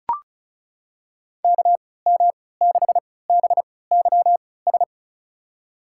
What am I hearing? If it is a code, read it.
Morse code, heard as a keyed tone, about 34 wpm, spaced as for 20 wpm, 700 Hz. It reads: KM6BYS
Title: USA Call Signs - 6 Characters - 3x Character Spacing - 35wpm